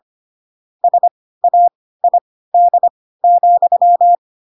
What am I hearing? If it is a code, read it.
Morse code, heard as a keyed tone, about 25 wpm, spaced as for 16 wpm, 700 Hz. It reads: SAID,
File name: Pinocchio / Pinocchio ch1 10 25 ik1hge